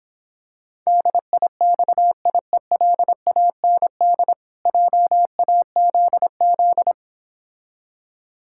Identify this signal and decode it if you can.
Morse code, heard as a keyed tone, about 26 wpm, 700 Hz. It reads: DIXIELAND JAZZ